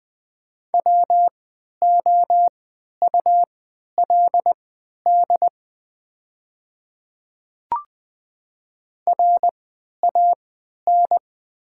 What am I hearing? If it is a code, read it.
Morse code, heard as a keyed tone, about 20 wpm, spaced as for 11 wpm, 700 Hz. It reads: WOULD RAN
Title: 3x Character Spacing - Top 500 Words - 20wpm